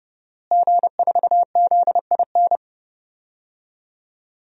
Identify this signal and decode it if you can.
Morse code, heard as a keyed tone, about 30 wpm, 700 Hz. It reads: G4ZIN
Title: DXCC-100 Call Signs - 5 Characters - Repeated 2x - 30wpm